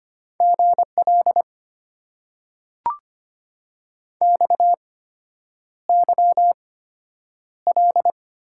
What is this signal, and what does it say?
Morse code, heard as a keyed tone, about 25 wpm, spaced as for 9 wpm, 700 Hz. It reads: GL XYL
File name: Top 90 QSO Elements - 8x Character Spacing - 25wpm